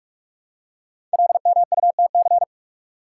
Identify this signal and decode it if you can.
Morse code, heard as a keyed tone, about 44 wpm, 690 Hz. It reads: LMUTC
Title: Code Groups - 5 Letters - 45wpm